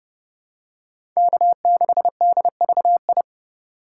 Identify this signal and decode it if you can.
Morse code, heard as a keyed tone, about 30 wpm, 700 Hz. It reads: K6DVI
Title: USA Call Signs - 3x Character Spacing - 30wpm